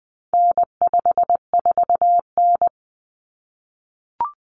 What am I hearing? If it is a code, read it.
Morse code, heard as a keyed tone, about 20 wpm, 700 Hz. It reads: N54N
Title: Common Cut Numbers - 4-Digit 20wpm